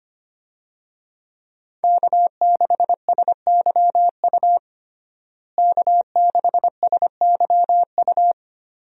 Morse code, heard as a keyed tone, about 25 wpm, 700 Hz. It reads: K6SYU K6SYU
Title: USA Call Signs - 5 Characters - Repeated 2x - 25wpm